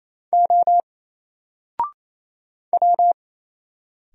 Morse code, heard as a keyed tone, about 27 wpm, 700 Hz. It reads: O W
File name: Single-Letter-Number-28wpm